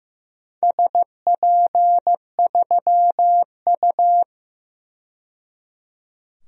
Morse code, heard as a keyed tone, about 15 wpm, 700 Hz. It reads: SP3U
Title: DXCC-100 Call Signs - 4 Characters - 15wpm